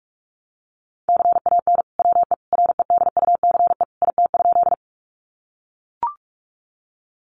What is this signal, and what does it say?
Morse code, heard as a keyed tone, about 44 wpm, 700 Hz. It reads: CAN WE REDUCE IT?
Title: Sentences from Top 1100 Words 45wpm